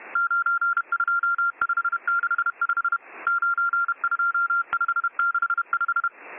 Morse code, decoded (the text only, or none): 92565 92565